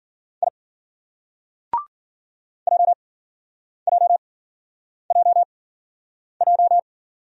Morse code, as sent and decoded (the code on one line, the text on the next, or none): .. / .--- .--- .--- .---
I JJJJ